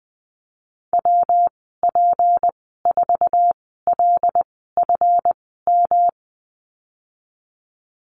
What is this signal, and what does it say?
Morse code, heard as a keyed tone, about 20 wpm, spaced as for 14 wpm, 700 Hz. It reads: WP4LFM
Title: 167 Lesson 27g - Number 7 - Call Signs 20wpm